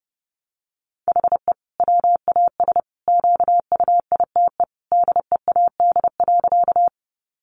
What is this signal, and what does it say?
Morse code, heard as a keyed tone, about 30 wpm, 700 Hz. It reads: HE WAS QUITE DEAD.
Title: Sentences from Top 1500 Words 30wpm